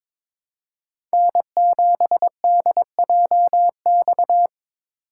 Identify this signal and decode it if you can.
Morse code, heard as a keyed tone, about 21 wpm, 700 Hz. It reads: N7DJX
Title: USA Call Signs - 4x Character Spacing - 22wpm